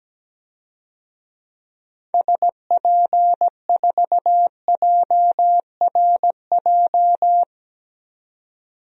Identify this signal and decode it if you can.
Morse code, heard as a keyed tone, about 17 wpm, 700 Hz. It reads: SP4JRJ
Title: DXCC-100 Call Signs - 6 Characters - 17wpm